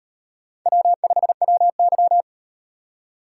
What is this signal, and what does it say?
Morse code, heard as a keyed tone, about 37 wpm, 700 Hz. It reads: W5WY